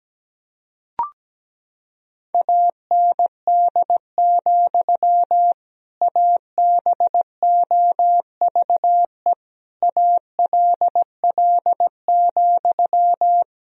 Morse code, heard as a keyed tone, about 17 wpm, 700 Hz. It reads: AND, ABOVE ALL,
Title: Sentences from Top 1000 Words - Review Part 2 - 17wpm